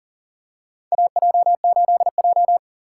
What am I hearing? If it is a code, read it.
Morse code, heard as a keyed tone, about 40 wpm, 700 Hz. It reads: AJ8J
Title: USA Call Signs - 4 Characters - 2x Character Spacing - 40wpm